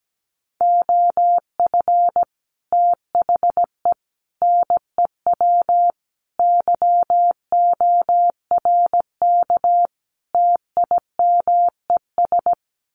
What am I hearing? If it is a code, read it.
Morse code, heard as a keyed tone, about 17 wpm, 700 Hz. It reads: OF THE NEW YORK TIMES
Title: Sets of 5 Words 17wpm